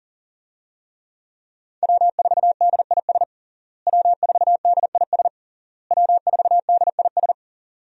Morse code, heard as a keyed tone, about 37 wpm, 700 Hz. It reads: W4DIS W4DIS W4DIS
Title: USA Call Signs - 5 Characters - Repeated 3x - 40wpm